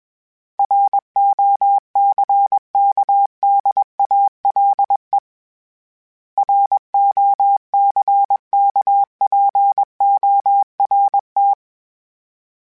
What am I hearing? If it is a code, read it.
Morse code, heard as a keyed tone, about 21 wpm, 800 Hz. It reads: ROCKDALE ROCKPORT
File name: fromSMs / FromSMs mp3s / Cities@20